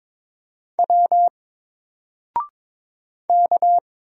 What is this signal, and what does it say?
Morse code, heard as a keyed tone, about 22 wpm, 700 Hz. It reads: W K
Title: Single Character (Letter:Number:Punctuation:Prosign) Alt 22wpm